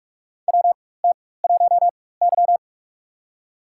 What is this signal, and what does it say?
Morse code, heard as a keyed tone, about 42 wpm, spaced as for 20 wpm, 700 Hz. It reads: WT1Y
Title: USA Call Signs - 4 Characters - 4x Character Spacing - 45wpm